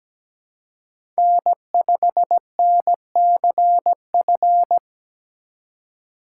Morse code, heard as a keyed tone, about 17 wpm, 700 Hz. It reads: N5NCF
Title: USA Call Signs - 5 Characters - 2x Character Spacing - 17wpm